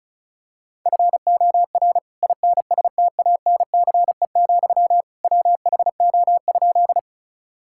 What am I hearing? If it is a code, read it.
Morse code, heard as a keyed tone, about 34 wpm, 700 Hz. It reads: FOR INSTANCE, WHO?
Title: Sentences from Top 1000 Words - Review Part 5 - 35wpm